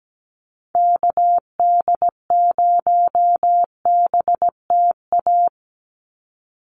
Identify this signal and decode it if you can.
Morse code, heard as a keyed tone, about 17 wpm, 700 Hz. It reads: KD0BTA
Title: USA Call Signs 17wpm